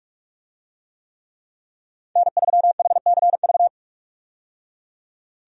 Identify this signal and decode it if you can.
Morse code, heard as a keyed tone, about 46 wpm, 700 Hz. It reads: N3HCV
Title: USA Call Signs - 5 Characters - 45wpm